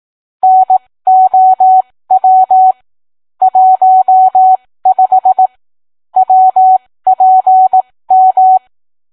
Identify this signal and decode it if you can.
Morse code, heard as a keyed tone, about 18 wpm, spaced as for 15 wpm, 750 Hz. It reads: NOW 15 WPM